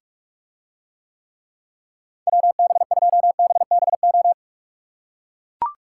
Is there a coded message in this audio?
Yes: Morse code from 2.3 to 4.3 s, heard as a keyed tone, about 44 wpm, 700 Hz. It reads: WB2BBO